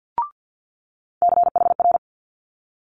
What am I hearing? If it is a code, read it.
Morse code, heard as a keyed tone, about 50 wpm, 700 Hz. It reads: CHR
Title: Multi-letter Phonemes Quick Fire 50wpm